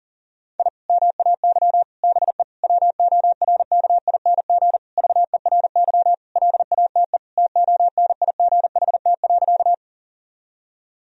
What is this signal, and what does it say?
Morse code, heard as a keyed tone, about 40 wpm, 700 Hz. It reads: I MAY BE WORKING VERY LATE TONIGHT.
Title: Sentences from Top 1000 Words - Review Part 4 - 40wpm